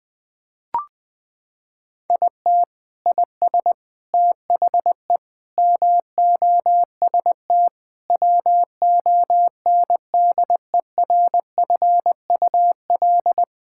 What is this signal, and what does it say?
Morse code, heard as a keyed tone, about 20 wpm, 700 Hz. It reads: IT IS THE MOST WONDERFUL
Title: Sentences with 8-Word Phrases - 20wpm